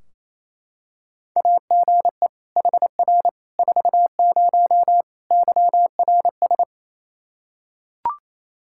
Morse code, heard as a keyed tone, about 28 wpm, 700 Hz. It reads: AGE HR 40 YRS
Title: Practice QSO - Exchange 3 - More Info - 28wpm